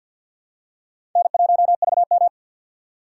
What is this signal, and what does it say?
Morse code, heard as a keyed tone, about 48 wpm, 700 Hz. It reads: N1VM
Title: USA Call Signs - 4 Characters - 2x Character Spacing - 50wpm